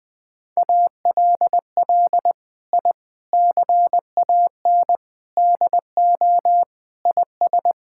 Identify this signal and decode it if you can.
Morse code, heard as a keyed tone, about 21 wpm, 700 Hz. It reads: ALL I CAN DO IS
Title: Sentences from Top 1000 Words - Review Part 6 - 20wpm